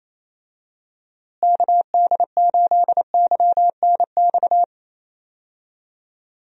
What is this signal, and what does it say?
Morse code, heard as a keyed tone, about 27 wpm, 700 Hz. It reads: KD8YNX
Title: USA Call Signs - 6 Characters - Repeated 2x - 28wpm